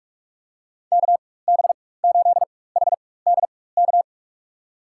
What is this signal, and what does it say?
Morse code, heard as a keyed tone, about 45 wpm, spaced as for 21 wpm, 700 Hz. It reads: KB8HDK
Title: USA Call Signs - 6 Characters - 4x Character Spacing - 45wpm